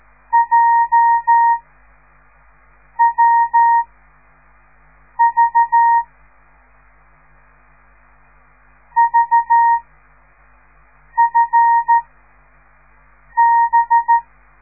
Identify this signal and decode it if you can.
Morse code, heard as a keyed tone, about 13 wpm, 930 Hz. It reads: J W V V F B